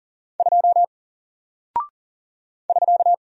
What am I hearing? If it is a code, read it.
Morse code, heard as a keyed tone, about 38 wpm, 700 Hz. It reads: 2 <SK>
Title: Single Character (Letter:Number:Punctuation:Prosign) Alt 40wpm